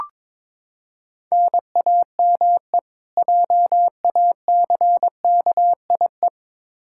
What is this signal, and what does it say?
Morse code, heard as a keyed tone, about 22 wpm, 700 Hz. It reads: NAME JACKIE
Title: Practice QSO - Exchange 1 - Beginning - 22wpm